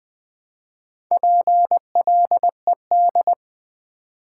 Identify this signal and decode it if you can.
Morse code, heard as a keyed tone, about 20 wpm, 700 Hz. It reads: PLED
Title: Build A Word - 2 Letters At A Time - 6-Letter Words - 20wpm